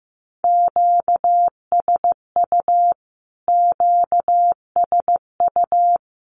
Morse code, heard as a keyed tone, about 15 wpm, 700 Hz. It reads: QSU QSU
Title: Q-Codes Repeated 3x 15wpm